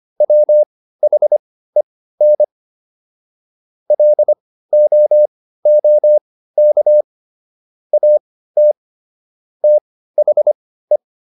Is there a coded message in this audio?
Yes: Morse code from 0.2 to 11.0 s, heard as a keyed tone, about 25 wpm, spaced as for 15 wpm, 600 Hz. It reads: WHEN LOOK AT THE